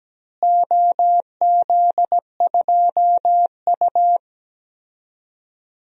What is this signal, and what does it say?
Morse code, heard as a keyed tone, about 17 wpm, 700 Hz. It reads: OZ2U
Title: DXCC-100 Call Signs - 4 Characters - Repeated 2x - 17wpm